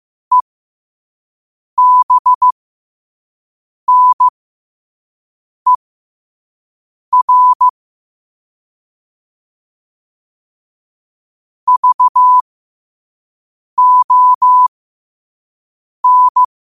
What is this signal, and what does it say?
Morse code, heard as a keyed tone, about 15 wpm, spaced as for 5 wpm, 1000 Hz. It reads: EBNER VON
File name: qotdfiles / 5wpm